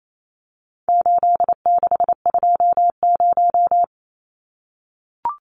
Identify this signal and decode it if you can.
Morse code, heard as a keyed tone, about 28 wpm, 700 Hz. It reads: 8620
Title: Numbers - 4-Digit 28wpm